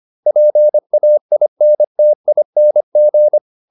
Morse code, heard as a keyed tone, about 25 wpm, 600 Hz. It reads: PAINTING